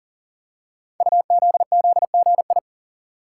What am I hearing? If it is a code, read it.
Morse code, heard as a keyed tone, about 40 wpm, 700 Hz. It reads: UZZGI
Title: Code Groups - 5 Letters - 40wpm